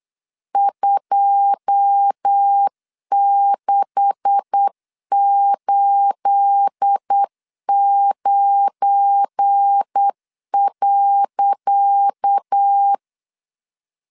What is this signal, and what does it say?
Morse code, heard as a keyed tone, about 8 wpm, 800 Hz. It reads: 2689.
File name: LEZIONI / 10 SIGNS / 10C DOT 8wpm 3649